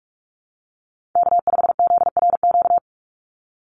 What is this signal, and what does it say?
Morse code, heard as a keyed tone, about 44 wpm, 700 Hz. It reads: K5ZRQ
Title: USA Call Signs 45wpm